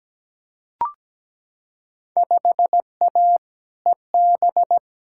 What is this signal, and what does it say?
Morse code, heard as a keyed tone, about 17 wpm, 700 Hz. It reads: 5A EB
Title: Field Day Exchanges Encore - 17wpm